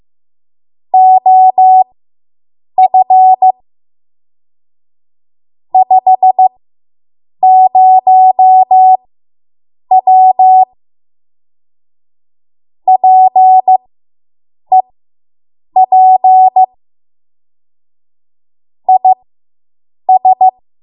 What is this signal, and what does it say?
Morse code, heard as a keyed tone, about 15 wpm, spaced as for 7 wpm, 750 Hz. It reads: OF 50W PEP IS